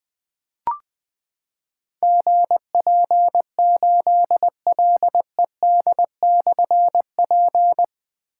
Morse code, heard as a keyed tone, about 20 wpm, 700 Hz. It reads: GP8LED/P
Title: Difficult Call Signs 20wpm